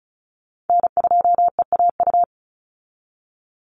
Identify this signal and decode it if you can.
Morse code, heard as a keyed tone, about 36 wpm, 700 Hz. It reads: N2EAU
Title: USA Call Signs 35wpm